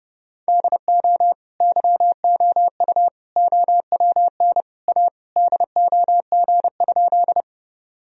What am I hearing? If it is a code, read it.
Morse code, heard as a keyed tone, about 30 wpm, 700 Hz. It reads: DO YOU OWN A DOG?
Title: Sentences from Top 1000 Words - Review Part 2 - 30wpm